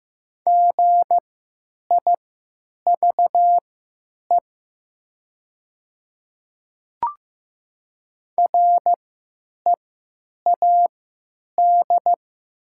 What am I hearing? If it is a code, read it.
Morse code, heard as a keyed tone, about 15 wpm, spaced as for 9 wpm, 700 Hz. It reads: GIVE READ
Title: 3x Character Spacing - Top 200 Words - 15wpm